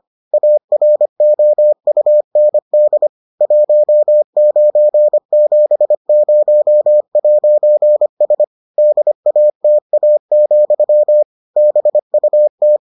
Morse code, heard as a keyed tone, about 25 wpm, 600 Hz. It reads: AROUND 1970'S DATA, BUT